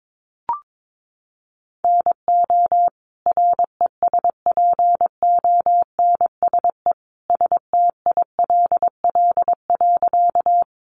Morse code, heard as a keyed tone, about 22 wpm, 700 Hz. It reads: NO RESPONSE STILL.
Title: Sentences from Top 1400 Words 22wpm